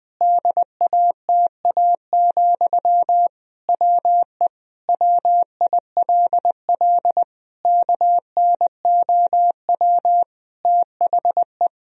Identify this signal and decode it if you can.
Morse code, heard as a keyed tone, about 20 wpm, 700 Hz. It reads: DATA, WE WILL KNOW THE
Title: Sentences from Top 1000 Words - Review Part 3 - 20wpm